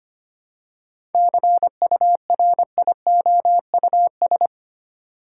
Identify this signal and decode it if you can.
Morse code, heard as a keyed tone, about 25 wpm, 700 Hz. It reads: CURIOUS
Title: Seven Letter Words - 25wpm